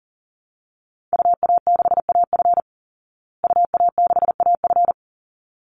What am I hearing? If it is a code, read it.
Morse code, heard as a keyed tone, about 40 wpm, 700 Hz. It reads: UA6AF UA6AF
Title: Call Signs Repeated 3x - 40wpm